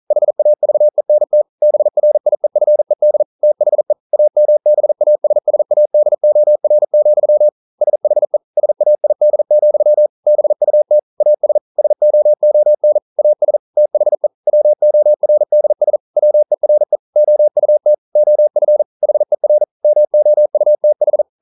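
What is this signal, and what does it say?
Morse code, heard as a keyed tone, about 42 wpm, 600 Hz. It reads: HAVENT BRIEFED THE AMBASSADOR, SHE SAID, BUT AS SOON AS THE WORDS WERE OUT OF HER MOUTH
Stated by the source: Morse code practice at 41 wpm, from CW med Gnister.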